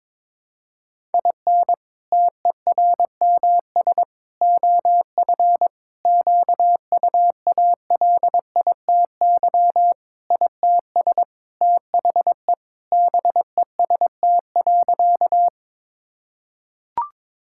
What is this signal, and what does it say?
Morse code, heard as a keyed tone, about 22 wpm, 700 Hz. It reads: IN TERMS OF QUALITY ITS THE BEST.
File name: practice / 22wpm / Sentences with 3-Word Phrases - 22wpm